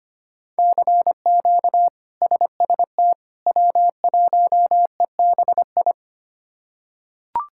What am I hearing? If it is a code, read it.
Morse code, heard as a keyed tone, about 25 wpm, 700 Hz. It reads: CQ SST W1EBI
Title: K1USN SST QSO Practice - 25wpm